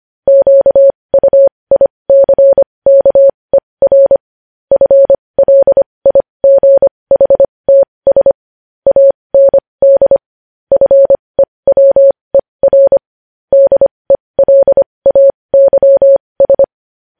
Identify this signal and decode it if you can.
Morse code, heard as a keyed tone, about 25 wpm, spaced as for 20 wpm, 550 Hz. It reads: QUICKER FLIGHTS AND FEWER DELAYS
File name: podcast / News headlines-1x-25x20